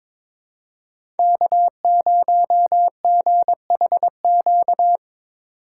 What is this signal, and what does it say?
Morse code, heard as a keyed tone, about 22 wpm, 700 Hz. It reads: K0GHQ
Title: USA Call Signs - 5 Characters - 2x Character Spacing - 22wpm